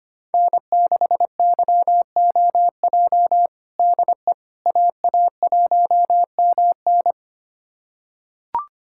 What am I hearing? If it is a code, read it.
Morse code, heard as a keyed tone, about 25 wpm, 700 Hz. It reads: N6YOJ DE AA1MN
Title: USA Callsign DE Callsign - 5 Characters - 25wpm